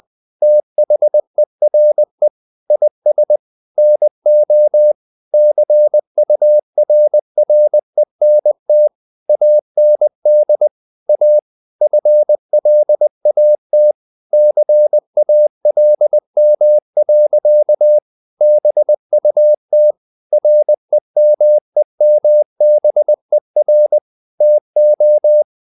Morse code, heard as a keyed tone, about 20 wpm, 600 Hz. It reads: THERE IS NO CURRENT AND A FLAT CALM. BUT REMEMBER TO
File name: the old man and the sea part 2 20wpm